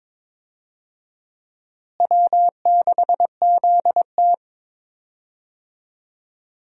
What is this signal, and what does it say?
Morse code, heard as a keyed tone, about 22 wpm, 700 Hz. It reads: W6ZT
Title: USA Call Signs - 4 Characters - Repeated 2x - 22wpm